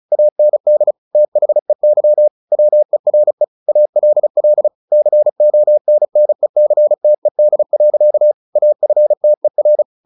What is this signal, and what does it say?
Morse code, heard as a keyed tone, about 34 wpm, 600 Hz. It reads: AND THEY WERE ALL CONNECTED. AFTER
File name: the old man and the sea / the old man and the sea 1 35wpm